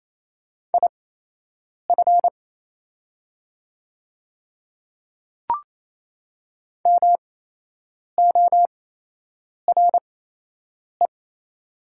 Morse code, heard as a keyed tone, about 28 wpm, spaced as for 8 wpm, 700 Hz. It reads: IF MORE
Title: 8x Character Spacing - Top 500 Words - 28wpm